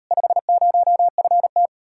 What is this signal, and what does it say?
Morse code, heard as a keyed tone, about 40 wpm, 700 Hz. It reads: 50FT